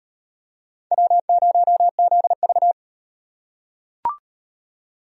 Morse code, heard as a keyed tone, about 38 wpm, 700 Hz. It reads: W0ZV